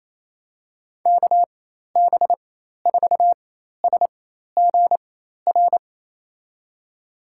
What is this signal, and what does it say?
Morse code, heard as a keyed tone, about 28 wpm, spaced as for 13 wpm, 700 Hz. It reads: KB4SGR
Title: USA Call Signs - 4x Character Spacing - 28wpm